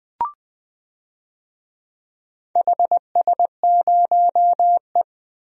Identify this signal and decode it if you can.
Morse code, heard as a keyed tone, about 21 wpm, 700 Hz. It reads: HS0E
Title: DXCC-100 Call Signs - 4 Characters - Repeated 3x - 20wpm